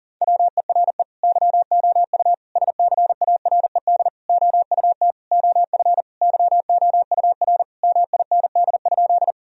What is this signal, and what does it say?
Morse code, heard as a keyed tone, about 40 wpm, 700 Hz. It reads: WERE YOU SCARED OUT OF YOUR MIND?
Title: Sentences from Top 1000 Words - Review Part 4 - 40wpm